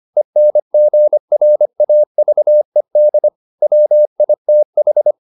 Morse code, heard as a keyed tone, about 25 wpm, 600 Hz. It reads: ENGRAVED WITH